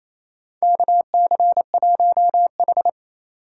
Morse code, heard as a keyed tone, about 28 wpm, 700 Hz. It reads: KC1H
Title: USA Call Signs - 4 Characters - 4x Character Spacing - 28wpm